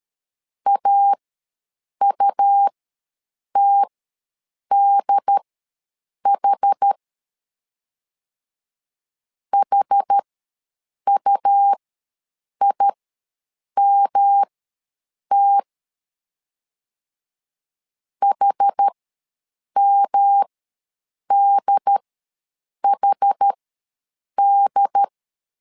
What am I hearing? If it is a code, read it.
Morse code, heard as a keyed tone, about 13 wpm, spaced as for 7 wpm, 800 Hz. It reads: AUTDH HUIMT HMDHD
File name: LEZIONI / 3 MWRO / 3 M ws 12wpm 1745